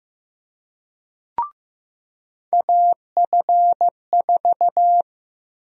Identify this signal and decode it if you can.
Morse code, heard as a keyed tone, about 15 wpm, 700 Hz. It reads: AF4
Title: USA Call Sign Prefixes - Repeated 2x - 15wpm